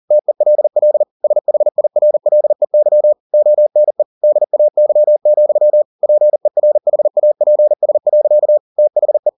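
Morse code, heard as a keyed tone, about 40 wpm, 600 Hz. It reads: TELL SHIRLEY ONE DAY, PERHAPS. THE
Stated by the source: Morse code at 40 wpm, practice text, CW med Gnister.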